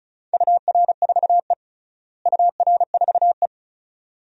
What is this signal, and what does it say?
Morse code, heard as a keyed tone, about 36 wpm, 700 Hz. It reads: UR4E UR4E
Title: DXCC-100 Call Signs - 4 Characters - Repeated 2x - 35wpm